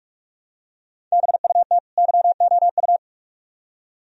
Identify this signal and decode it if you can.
Morse code, heard as a keyed tone, about 46 wpm, 700 Hz. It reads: BUT YOU